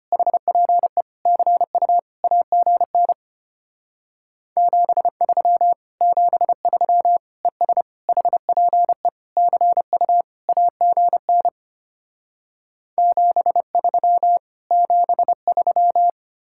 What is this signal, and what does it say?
Morse code, heard as a keyed tone, about 33 wpm, 700 Hz. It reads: HPE CU AGN 73 73 ES HPE CU AGN 73 73